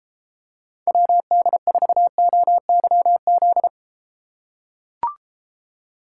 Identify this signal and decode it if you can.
Morse code, heard as a keyed tone, about 33 wpm, 700 Hz. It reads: WD4OYZ